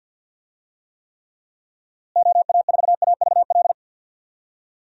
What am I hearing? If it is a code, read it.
Morse code, heard as a keyed tone, about 46 wpm, 700 Hz. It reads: OA4AVL
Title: USA Call Signs - 6 Characters - Repeated 3x - 50wpm